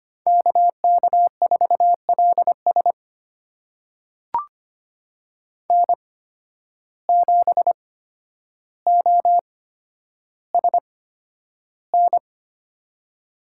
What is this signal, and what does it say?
Morse code, heard as a keyed tone, about 25 wpm, spaced as for 11 wpm, 700 Hz. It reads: KK4LS N7OSN
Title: USA Call Signs - 5 Characters - 8x Character Spacing - 25wpm